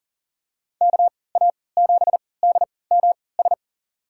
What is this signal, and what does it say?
Morse code, heard as a keyed tone, about 40 wpm, spaced as for 23 wpm, 700 Hz. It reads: KA7DMS
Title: USA Call Signs - 6 Characters - 3x Character Spacing - 40wpm